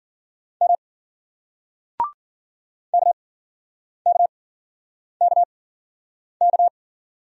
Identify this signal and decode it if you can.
Morse code, heard as a keyed tone, about 54 wpm, 700 Hz. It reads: M K K K K